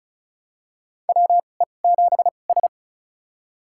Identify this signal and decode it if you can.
Morse code, heard as a keyed tone, about 36 wpm, spaced as for 25 wpm, 700 Hz. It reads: WE7S
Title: USA Call Signs - 4 Characters - 2x Character Spacing - 35wpm